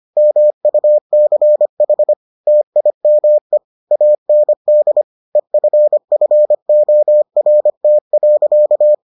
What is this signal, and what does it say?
Morse code, heard as a keyed tone, about 25 wpm, 600 Hz. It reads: MUCH TIME AND EFFORT.